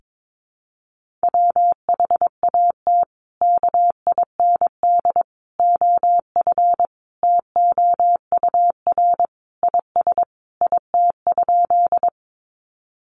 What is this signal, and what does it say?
Morse code, heard as a keyed tone, about 22 wpm, 700 Hz. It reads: WHAT KIND OF TOUR IS IT?
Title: Sentences from Top 1400 Words 22wpm